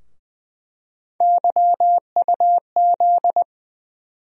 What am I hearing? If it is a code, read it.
Morse code, heard as a keyed tone, about 21 wpm, 700 Hz. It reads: YUZ
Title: USA Call Sign Postfixes - 3 Letters 20wpm